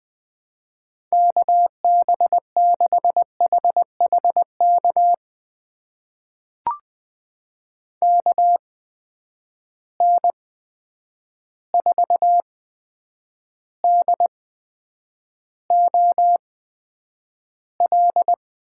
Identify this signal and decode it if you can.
Morse code, heard as a keyed tone, about 20 wpm, spaced as for 9 wpm, 700 Hz. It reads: KB6HHK KN4DOL
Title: USA Call Signs - 8x Character Spacing - 20wpm